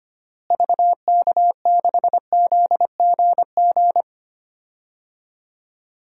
Morse code, heard as a keyed tone, about 25 wpm, 700 Hz. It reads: VK6ZGG
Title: DXCC-100 Call Signs - 6 Characters - Repeated 2x - 25wpm